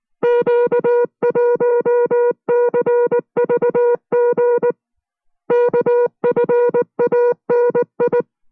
Morse code, heard as a keyed tone, about 19 wpm, 480 Hz. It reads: Q1CVG KFANI